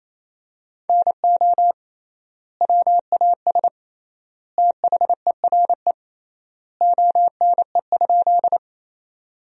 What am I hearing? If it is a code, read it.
Morse code, heard as a keyed tone, about 28 wpm, 700 Hz. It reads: NO WAS THERE ONE?
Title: 3x Word-Spacing - Sentences from Top 100 Words - 28wpm